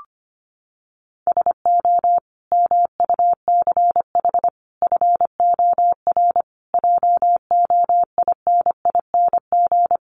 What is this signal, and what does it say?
Morse code, heard as a keyed tone, about 25 wpm, 700 Hz. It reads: SO MUCH FOR JOINING
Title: Sets of 5 Words 25wpm